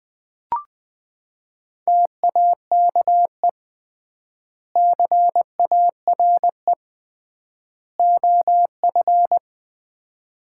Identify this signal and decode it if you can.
Morse code, heard as a keyed tone, about 20 wpm, 700 Hz. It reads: TAKE CARE OF
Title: 3x Word-Spacing - Sets of 3 Words - 20wpm